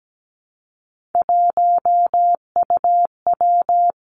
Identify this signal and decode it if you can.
Morse code, heard as a keyed tone, about 17 wpm, 700 Hz. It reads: 1UW
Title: Three Challenging Letter-Number Combinations 17wpm